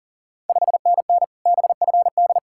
Morse code, heard as a keyed tone, about 40 wpm, 690 Hz. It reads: 5NN BFD
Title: Ontario QSO Party Exchange 40wpm